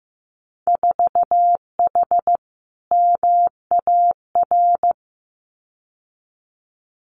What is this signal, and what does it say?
Morse code, heard as a keyed tone, about 15 wpm, 700 Hz. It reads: 4H MAR
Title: Winter Field Day Exchanges Encore 15wpm